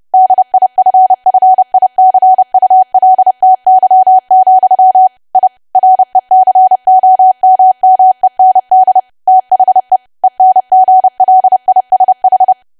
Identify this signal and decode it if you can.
Morse code, heard as a keyed tone, about 30 wpm, 750 Hz. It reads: DIFFICULTY, I RECOMMEND THE ENGLISH
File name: cwa / W1AW-Level3 / 30WPM